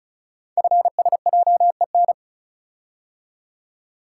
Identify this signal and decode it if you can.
Morse code, heard as a keyed tone, about 35 wpm, 690 Hz. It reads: FSJEN
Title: Code Groups - 5 Letters - 35wpm